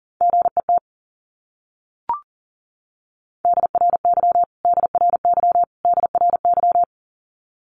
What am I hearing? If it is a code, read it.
Morse code, heard as a keyed tone, about 40 wpm, 700 Hz. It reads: GET DRY DRY DRY
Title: Top 500 Words Repeated 40wpm